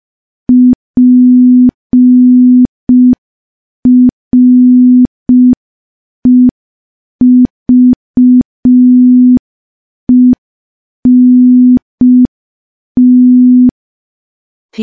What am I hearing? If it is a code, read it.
Morse code, heard as a keyed tone, about 5 wpm, 250 Hz. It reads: PREVENT
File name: morse / morse-training-05WPM